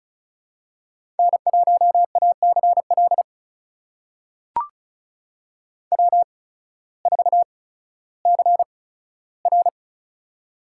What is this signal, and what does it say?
Morse code, heard as a keyed tone, about 36 wpm, spaced as for 16 wpm, 700 Hz. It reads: N1ACL W4CR
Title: USA Call Signs - 8x Character Spacing - 35wpm